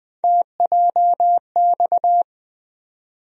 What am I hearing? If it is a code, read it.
Morse code, heard as a keyed tone, about 20 wpm, 700 Hz. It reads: TJX
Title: USA Call Sign Postfixes - 3 Letters 20wpm